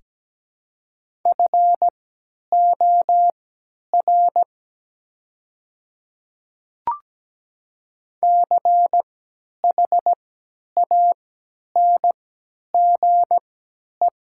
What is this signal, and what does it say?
Morse code, heard as a keyed tone, about 17 wpm, spaced as for 10 wpm, 700 Hz. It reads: FOR CHANGE
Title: 3x Character Spacing - Top 200 Words - 17wpm